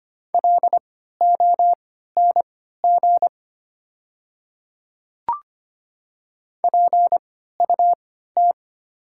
Morse code, heard as a keyed tone, about 25 wpm, spaced as for 14 wpm, 700 Hz. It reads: LONG PUT
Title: 3x Character Spacing - Top 300 Words - 25wpm